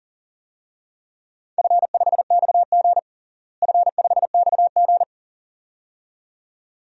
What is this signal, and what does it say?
Morse code, heard as a keyed tone, about 40 wpm, 700 Hz. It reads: F5XG F5XG
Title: DXCC-100 Call Signs - 4 Characters - Repeated 2x - 40wpm